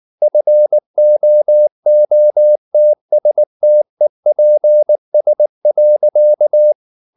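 Morse code, heard as a keyed tone, about 19 wpm, 600 Hz. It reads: FOOTSTEPS.
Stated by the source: Morse code at 19 wpm from CW med Gnister, reading Ken Follett's novel Never in English.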